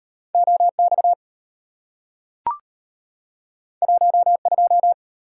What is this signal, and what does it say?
Morse code, heard as a keyed tone, about 34 wpm, 700 Hz. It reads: OX 12